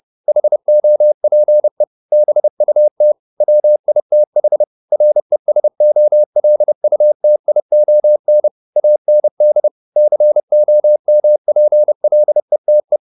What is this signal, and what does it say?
Morse code, heard as a keyed tone, about 30 wpm, 600 Hz. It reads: HOPE BUT WITH RESOLUTION AND COMPLETE